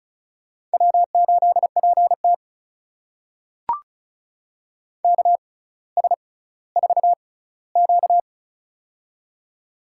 Morse code, heard as a keyed tone, about 36 wpm, spaced as for 18 wpm, 700 Hz. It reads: W8PT KS4Q
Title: USA Call Signs - 4 Characters - 6x Character Spacing - 35wpm